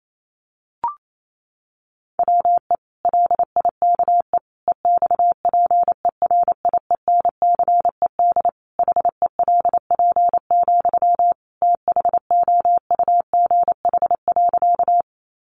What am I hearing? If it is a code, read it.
Morse code, heard as a keyed tone, about 28 wpm, 700 Hz. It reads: WE LIKE EXPERIENCED HELP, THOUGH.
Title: Sentences from Top 1100 Words 28wpm